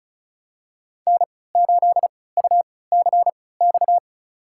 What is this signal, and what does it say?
Morse code, heard as a keyed tone, about 36 wpm, spaced as for 20 wpm, 700 Hz. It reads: N8UCX
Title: USA Call Signs - 5 Characters - 3x Character Spacing - 35wpm